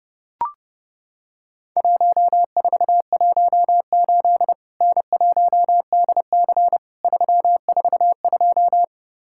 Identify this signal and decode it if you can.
Morse code, heard as a keyed tone, about 30 wpm, 700 Hz. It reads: 1418 N1DC 342
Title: Worked All Europe DX Contest - QTC Exchange - 30wpm